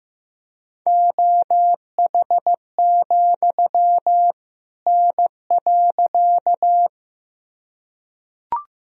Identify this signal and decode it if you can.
Morse code, heard as a keyed tone, about 15 wpm, 700 Hz. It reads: OH, N.
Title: Sentences from Top 500 Words and Abbreviations 15wpm